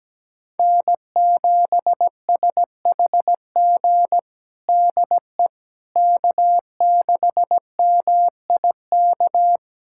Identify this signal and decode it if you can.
Morse code, heard as a keyed tone, about 17 wpm, 700 Hz. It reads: N7SHG DE K6MIK
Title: USA Callsign DE Callsign - 5 Characters - 17wpm